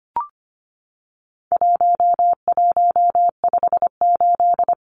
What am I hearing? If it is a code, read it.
Morse code, heard as a keyed tone, about 25 wpm, 700 Hz. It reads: 1158
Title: Numbers - 4-Digit 25wpm